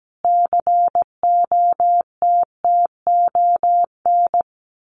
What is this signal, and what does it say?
Morse code, heard as a keyed tone, about 17 wpm, 700 Hz. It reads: COTTON
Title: A Sound Of Thunder 17wpm